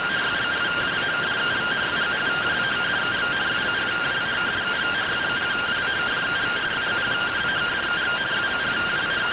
Two-tone teleprinter signal, about 170 Hz shift.
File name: findings / MR AXIAL fr1 120.gen-37 rtty rx